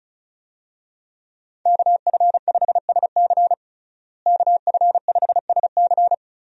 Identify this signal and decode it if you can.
Morse code, heard as a keyed tone, about 34 wpm, 700 Hz. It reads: KF5SC KF5SC
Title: USA Call Signs - 5 Characters - Repeated 2x - 35wpm